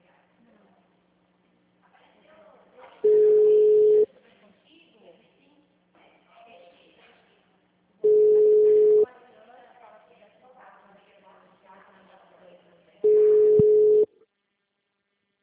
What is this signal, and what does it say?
Ringback tone (Europe, 425 Hz), with speech.
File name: out- -2007- - - .1457